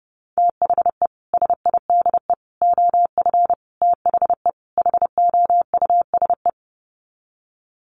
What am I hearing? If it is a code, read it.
Morse code, heard as a keyed tone, about 30 wpm, 700 Hz. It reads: THE SIDE OF THE HOUSE
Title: Sets of 5 Words Encore2 30wpm